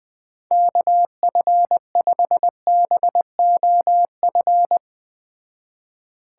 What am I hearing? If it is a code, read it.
Morse code, heard as a keyed tone, about 20 wpm, 700 Hz. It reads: KF5BOF
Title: Speed Racing - USA 2x3 Call Signs 17wpm